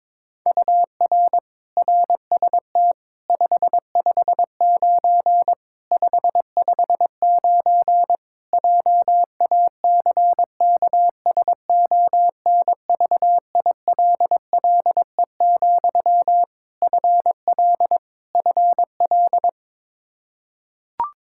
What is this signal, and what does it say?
Morse code, heard as a keyed tone, about 22 wpm, 700 Hz. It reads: UR RST 559 559 JACKSONVILLE, FL FL
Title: Practice QSO - Exchange 1 - Beginning - 22wpm